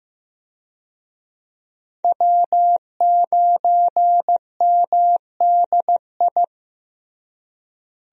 Morse code, heard as a keyed tone, about 16 wpm, 700 Hz. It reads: W9MDI
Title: USA Call Signs - 5 Characters - Repeated 3x - 15wpm